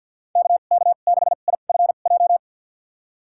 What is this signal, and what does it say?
Morse code, heard as a keyed tone, about 52 wpm, spaced as for 37 wpm, 690 Hz. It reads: KK6IFJ
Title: USA Call Signs - 2x Character Spacing - 50wpm